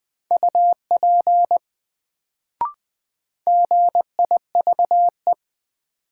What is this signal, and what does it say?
Morse code, heard as a keyed tone, about 20 wpm, 700 Hz. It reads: UP GIVE
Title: Sets of 2 Words - New Word Drill - 20wpm